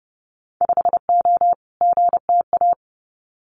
Morse code, heard as a keyed tone, about 30 wpm, 700 Hz. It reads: 5O GTA
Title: Winter Field Day Exchanges Encore 30wpm